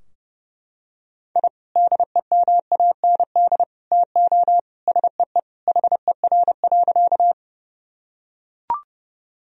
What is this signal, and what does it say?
Morse code, heard as a keyed tone, about 30 wpm, 700 Hz. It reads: I DEMAND TO SEE HER.
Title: Sentences from Top 1000 Words - Review Part 4 - 30wpm